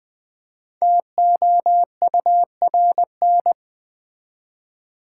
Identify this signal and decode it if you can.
Morse code, heard as a keyed tone, about 20 wpm, 700 Hz. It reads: TOURN
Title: Build A Word - 5 Letters At A Time - 10-Letter Words - 20wpm